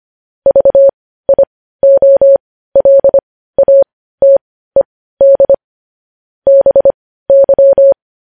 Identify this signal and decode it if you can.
Morse code, heard as a keyed tone, about 25 wpm, spaced as for 15 wpm, 550 Hz. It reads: VIOLATED BY
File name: podcast / News headlines-1x-25x15